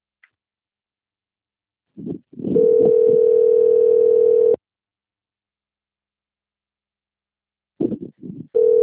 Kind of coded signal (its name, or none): telephone call progress tone